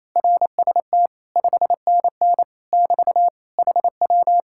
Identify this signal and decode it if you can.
Morse code, heard as a keyed tone, about 26 wpm, 700 Hz. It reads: RST 5NN <BT> HW
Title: Speed-Racing Common QSO Phrases - 28wpm